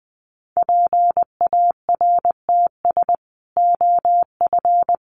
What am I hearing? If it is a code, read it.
Morse code, heard as a keyed tone, about 20 wpm, 700 Hz. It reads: PARTS OF
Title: Sets of 3 Words 20wpm